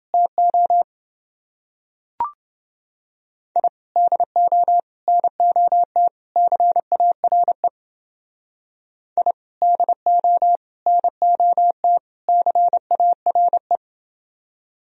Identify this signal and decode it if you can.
Morse code, heard as a keyed tone, about 28 wpm, 700 Hz. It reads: TO I DO NOT CARE I DO NOT CARE